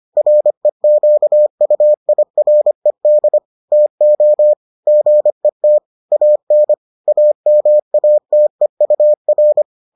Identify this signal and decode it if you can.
Morse code, heard as a keyed tone, about 25 wpm, 600 Hz. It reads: REQUIRED TO GET AN AMATEUR